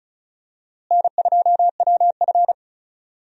Morse code, heard as a keyed tone, about 35 wpm, 700 Hz. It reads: N2WF
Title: USA Call Signs - 4 Characters - Repeated 2x - 35wpm